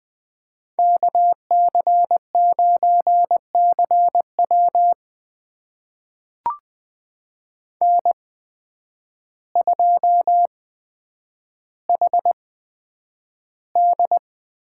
Morse code, heard as a keyed tone, about 20 wpm, spaced as for 9 wpm, 700 Hz. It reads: KC9CW N2HD
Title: USA Call Signs - 5 Characters - 8x Character Spacing - 20wpm